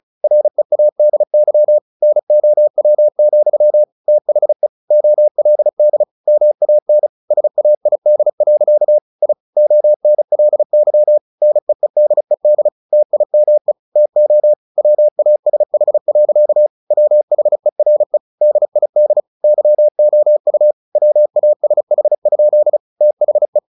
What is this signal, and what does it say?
Morse code, heard as a keyed tone, about 34 wpm, 600 Hz. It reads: READY NOW, THE OLD MAN SAID. I ONLY NEEDED TIME TO WASH. WHERE DID YOU WASH? THE
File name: the old man and the sea / the old man and the sea 1 35wpm